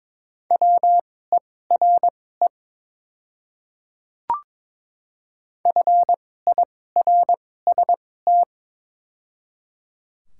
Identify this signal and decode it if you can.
Morse code, heard as a keyed tone, about 22 wpm, spaced as for 16 wpm, 690 Hz. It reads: WERE FIRST
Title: 2x Character Spacing - Top 200 Words - 22wpm